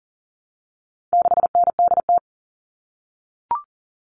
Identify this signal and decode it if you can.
Morse code, heard as a keyed tone, about 40 wpm, 700 Hz. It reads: 6NDT
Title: All Cut Numbers - 4-Digit 40wpm